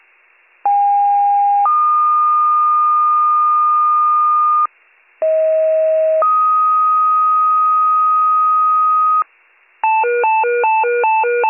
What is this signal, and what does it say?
Two-tone page, heard as a one-second tone, then a three-second tone, about 810 then 1220 Hz, about 640 then 1250 Hz. It is heard from 0.6 to 9.2 s.